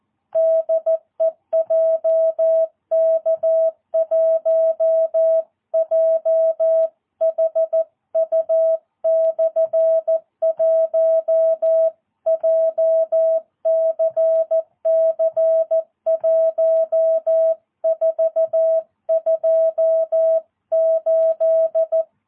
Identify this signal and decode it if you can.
Morse code, heard as a keyed tone, about 14 wpm, 660 Hz. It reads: DE JK1JHU/1 JCC 1428
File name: jh8cbh / 2022cw